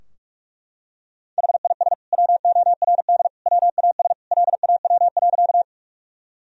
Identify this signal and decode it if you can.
Morse code, heard as a keyed tone, about 41 wpm, 700 Hz. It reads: HIS WORD WAS LAW.
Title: Sentences from Top 1000 Words - Review Part 3 - 45wpm